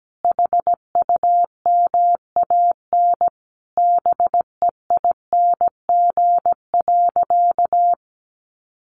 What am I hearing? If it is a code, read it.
Morse code, heard as a keyed tone, about 17 wpm, 700 Hz. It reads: HUMAN BEING.
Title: Sentences from Top 1500 Words 17wpm